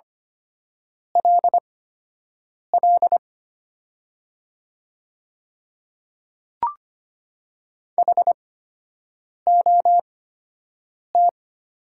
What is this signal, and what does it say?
Morse code, heard as a keyed tone, about 25 wpm, spaced as for 7 wpm, 700 Hz. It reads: LL HOT
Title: 8x Character Spacing - Top 100 Words - 25wpm